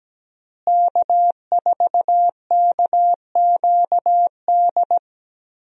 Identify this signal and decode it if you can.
Morse code, heard as a keyed tone, about 17 wpm, 700 Hz. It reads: K4KQD
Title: USA Call Signs - 5 Characters - Repeated 3x - 17wpm